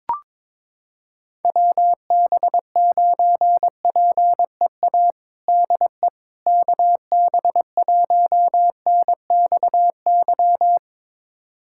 Morse code, heard as a keyed tone, about 22 wpm, 700 Hz. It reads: WB9PEA DE KB1NXY
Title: USA Callsign DE Callsign - 6 Characters - 22wpm